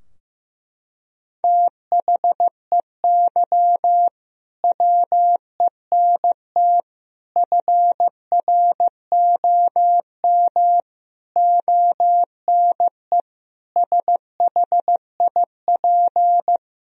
Morse code, heard as a keyed tone, about 16 wpm, 700 Hz. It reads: THEY WENT FROM ONE SHIP
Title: Sentences from Top 1000 Words - Review Part 2 - 15wpm